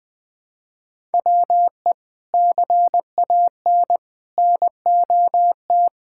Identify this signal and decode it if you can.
Morse code, heard as a keyed tone, about 19 wpm, 700 Hz. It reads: WE CAN NOT
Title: Sentences from Top 1000 Words - Review Part 6 - 20wpm